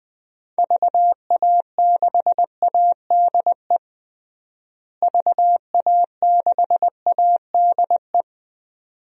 Morse code, heard as a keyed tone, about 21 wpm, 700 Hz. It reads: VA6ADE VA6ADE
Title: DXCC-100 Call Signs - 6 Characters - Repeated 2x - 20wpm